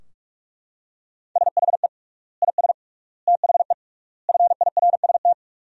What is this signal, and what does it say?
Morse code, heard as a keyed tone, about 42 wpm, 700 Hz. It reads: SHE IS THE FIRST
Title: 3x Word-Spacing - Sentences from Top 400 Words - 45wpm